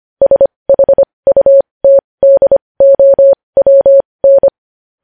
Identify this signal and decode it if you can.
Morse code, heard as a keyed tone, about 25 wpm, spaced as for 20 wpm, 550 Hz. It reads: SHUTDOWN
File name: podcast / News headlines-1x-25x20